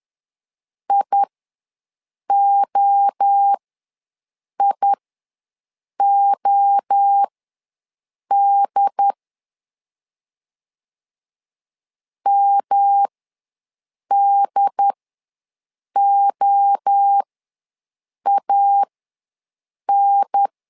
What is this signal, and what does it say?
Morse code, heard as a keyed tone, about 10 wpm, spaced as for 6 wpm, 800 Hz. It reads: IOIOD MDOAN
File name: LEZIONI / 3 MWRO / 3 MWRO ws 10wpm 2310